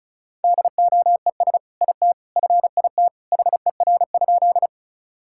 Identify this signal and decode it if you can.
Morse code, heard as a keyed tone, about 34 wpm, 700 Hz. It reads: DOES IT FIT HER?
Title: Sentences from Top 1000 Words - Review Part 5 - 35wpm